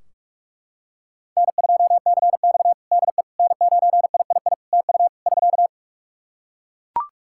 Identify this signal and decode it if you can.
Morse code, heard as a keyed tone, about 42 wpm, 700 Hz. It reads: N2CX DE N9III TU <SK>
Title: K1USN SST QSO Practice 45wpm